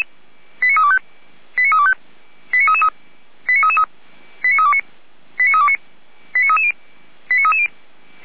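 Five-tone selective call (ZVEI), heard as quick runs of five tones, tones of about 70 ms. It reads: 89325 89325 89333 89329 89329 89330 89330
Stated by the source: ZVEI five-tone call 89508 received on an SDR, from the feuerwehr-orlamuende fire brigade server.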